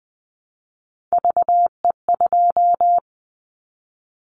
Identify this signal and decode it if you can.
Morse code, heard as a keyed tone, about 20 wpm, 700 Hz. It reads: VE2
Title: Callsign Prefixes for Top 100 DXCC - 20wpm